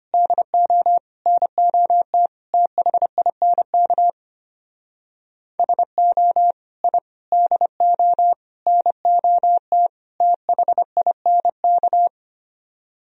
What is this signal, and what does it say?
Morse code, heard as a keyed tone, about 32 wpm, 700 Hz. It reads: DO NOT THINK SO I DO NOT THINK